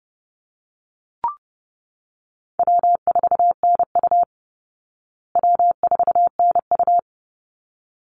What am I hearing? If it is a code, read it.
Morse code, heard as a keyed tone, about 30 wpm, 700 Hz. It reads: W4NU W4NU
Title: Easy Call Signs Repeated 2x - 30wpm